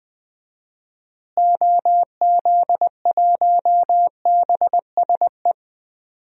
Morse code, heard as a keyed tone, about 20 wpm, 700 Hz. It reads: OZ1BSE
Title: DXCC-100 Call Signs - 6 Characters - Repeated 3x - 20wpm